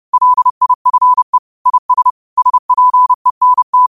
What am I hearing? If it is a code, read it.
Morse code, heard as a keyed tone, about 30 wpm, 1000 Hz. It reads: LIFE IS SPENT
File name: qotdfiles / 30wpm